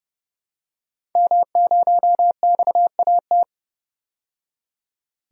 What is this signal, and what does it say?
Morse code, heard as a keyed tone, about 31 wpm, 700 Hz. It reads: M0XAT
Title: DXCC-100 Call Signs - 5 Characters - 30wpm